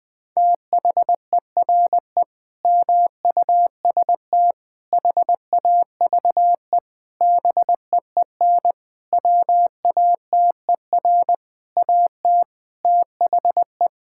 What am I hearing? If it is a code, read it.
Morse code, heard as a keyed tone, about 20 wpm, 700 Hz. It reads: THERE MUST HAVE BEEN WATER AT THE
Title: Sentences from Top 1000 Words - Review Part 6 - 20wpm